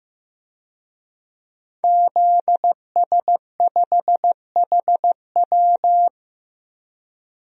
Morse code, heard as a keyed tone, about 15 wpm, 700 Hz. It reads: ZS5HW
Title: DXCC-100 Call Signs - 5 Characters - Repeated 3x - 15wpm